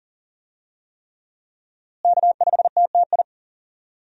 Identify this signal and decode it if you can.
Morse code, heard as a keyed tone, about 40 wpm, 700 Hz. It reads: K5TTI
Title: USA Call Signs - 5 Characters - Repeated 3x - 40wpm